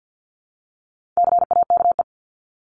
Morse code, heard as a keyed tone, about 50 wpm, 700 Hz. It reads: CAKE
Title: Top 1501-1750 Words 50wpm